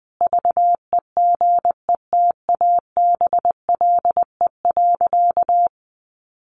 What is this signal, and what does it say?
Morse code, heard as a keyed tone, about 20 wpm, 700 Hz. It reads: VEGETABLE.
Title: Sentences from Top 1300 Words 20wpm